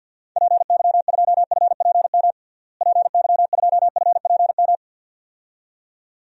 Morse code, heard as a keyed tone, about 50 wpm, 700 Hz. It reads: PY2FPM PY2FPM
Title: DXCC-100 Call Signs - 6 Characters - Repeated 2x - 50wpm